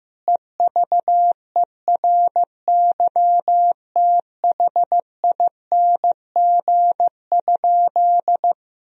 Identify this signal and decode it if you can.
Morse code, heard as a keyed tone, about 15 wpm, 700 Hz. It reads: EVERYTHING?
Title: Sentences from Top 1000 Words - Review Part 2 - 15wpm